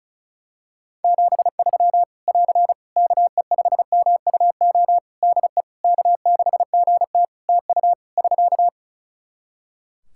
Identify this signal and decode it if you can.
Morse code, heard as a keyed tone, about 36 wpm, 700 Hz. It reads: 73 <AR> KE5MUO DE K6GT TU <SK>
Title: Practice QSO - Exchange 4 - Ending - 35wpm